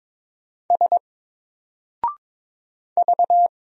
Morse code, heard as a keyed tone, about 22 wpm, 700 Hz. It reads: S V
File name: Single-Letter-Number-22wpm